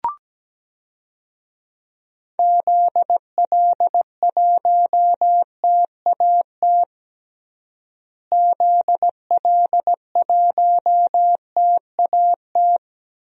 Morse code, heard as a keyed tone, about 17 wpm, 700 Hz. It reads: ZL1TAT ZL1TAT
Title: DXCC-100 Call Signs - 6 Characters - Repeated 3x - 17wpm